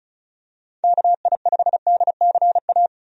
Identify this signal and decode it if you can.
Morse code, heard as a keyed tone, about 36 wpm, 700 Hz. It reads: KI5DCA
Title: USA Call Signs - 6 Characters - 35wpm